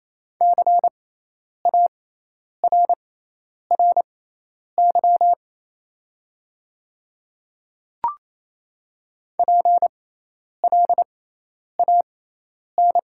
Morse code, heard as a keyed tone, about 28 wpm, spaced as for 10 wpm, 700 Hz. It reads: CARRY PLAN
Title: 6x Character Spacing - Top 500 Words - 28wpm